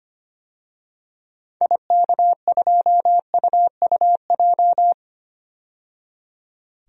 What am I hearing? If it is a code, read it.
Morse code, heard as a keyed tone, about 25 wpm, 700 Hz. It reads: IK2UUJ
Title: DXCC-100 Call Signs - 6 Characters - 25wpm